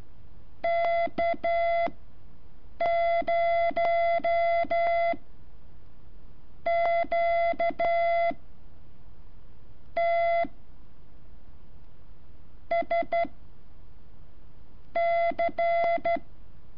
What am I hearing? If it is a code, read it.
Morse code, heard as a keyed tone, about 12 wpm, spaced as for 7 wpm, 680 Hz. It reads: K0 Q T S C
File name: curso / ejercicio 16